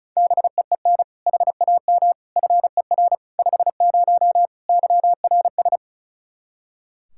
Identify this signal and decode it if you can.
Morse code, heard as a keyed tone, about 36 wpm, 700 Hz. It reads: BEEN HAM FER 50 YRS
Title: Practice QSO - Exchange 3 - More Info - 35wpm